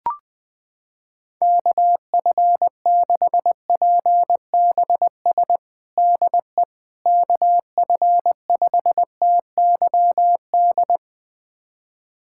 Morse code, heard as a keyed tone, about 20 wpm, 700 Hz. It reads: KF6PBS DE KF5TYD
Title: USA Callsign DE Callsign - 6 Characters - 20wpm